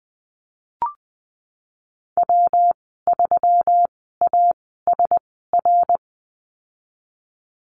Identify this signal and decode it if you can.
Morse code, heard as a keyed tone, about 20 wpm, spaced as for 15 wpm, 700 Hz. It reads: W3ASR
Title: 173 Lesson 28g - Number 9 - Call Signs 20wpm